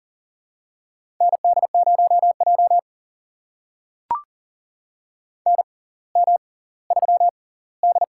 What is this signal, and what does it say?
Morse code, heard as a keyed tone, about 39 wpm, spaced as for 20 wpm, 690 Hz. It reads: ND0J NM3D
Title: USA Call Signs - 4 Characters - 6x Character Spacing - 40wpm